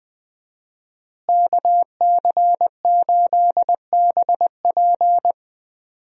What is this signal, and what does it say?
Morse code, heard as a keyed tone, about 20 wpm, 700 Hz. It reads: KC8BP
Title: USA Call Signs - 5 Characters - Repeated 2x - 20wpm